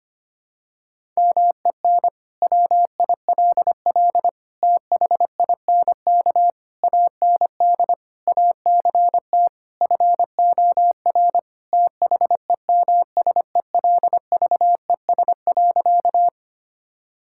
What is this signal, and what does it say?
Morse code, heard as a keyed tone, about 25 wpm, 700 Hz. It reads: MEN WILL THINK AND ACT FOR THEMSELVES.
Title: Sentences from Top 1000 Words - Review Part 5 - 25wpm